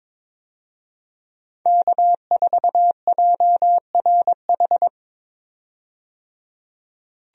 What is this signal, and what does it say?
Morse code, heard as a keyed tone, about 22 wpm, 700 Hz. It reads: K4JRH
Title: USA Call Signs - 5 Characters - 22wpm